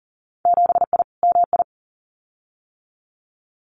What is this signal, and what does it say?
Morse code, heard as a keyed tone, about 38 wpm, 700 Hz. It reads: 7I MI
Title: Winter Field Day Exchanges Encore 40wpm